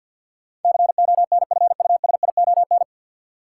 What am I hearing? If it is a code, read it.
Morse code, heard as a keyed tone, about 52 wpm, 700 Hz. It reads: CONFUSION
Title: Sentences from Top 1600 Words - New Word Drill Alt - 50wpm